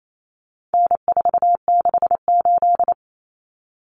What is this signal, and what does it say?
Morse code, heard as a keyed tone, about 28 wpm, 700 Hz. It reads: N468
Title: Common Cut Numbers - 4-Digit 28wpm